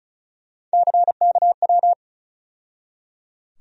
Morse code, heard as a keyed tone, about 34 wpm, 700 Hz. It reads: CKW
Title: USA Call Sign Postfixes - 3 Letters 35wpm